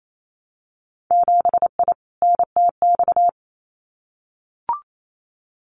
Morse code, heard as a keyed tone, about 28 wpm, 700 Hz. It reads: 7I NTX
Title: Winter Field Day Exchanges Encore 28wpm